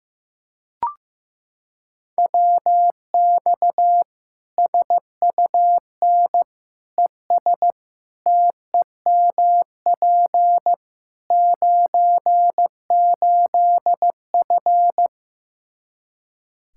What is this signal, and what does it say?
Morse code, heard as a keyed tone, about 15 wpm, 700 Hz. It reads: WX SUN ES TEMP 98F
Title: K1USN SST QSO Practice 15wpm